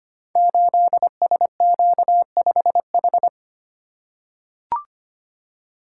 Morse code, heard as a keyed tone, about 25 wpm, 700 Hz. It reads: 8SQ5H
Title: Code Groups - 5 Letter:Numbers - 25wpm